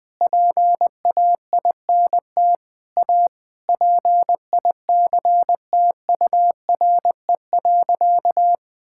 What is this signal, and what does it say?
Morse code, heard as a keyed tone, about 21 wpm, 700 Hz. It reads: PAINT A PICTURE.
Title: Sentences with 8-Word Phrases - 20wpm